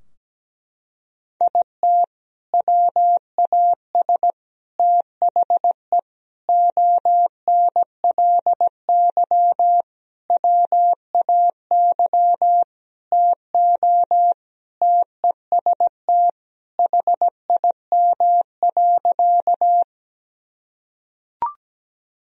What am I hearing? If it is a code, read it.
Morse code, heard as a keyed tone, about 17 wpm, 700 Hz. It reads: IT WAS THE ONLY WAY TO TEST HIM.
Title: Sentences from Top 1000 Words - Review Part 3 - 17wpm